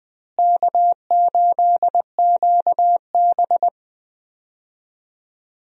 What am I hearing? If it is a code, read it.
Morse code, heard as a keyed tone, about 21 wpm, 700 Hz. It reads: K8QB
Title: USA Call Signs - 4 Characters - Repeated 3x - 20wpm